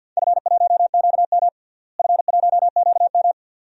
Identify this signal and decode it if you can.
Morse code, heard as a keyed tone, about 50 wpm, 700 Hz. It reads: F1QM F1QM
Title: DXCC-100 Call Signs - 4 Characters - Repeated 2x - 50wpm